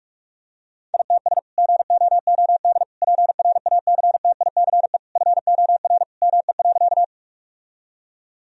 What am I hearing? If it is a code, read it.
Morse code, heard as a keyed tone, about 41 wpm, 700 Hz. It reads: ITS GOOD PRACTICE FOR ME.
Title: Sentences from Top 1000 Words - Review Part 4 - 45wpm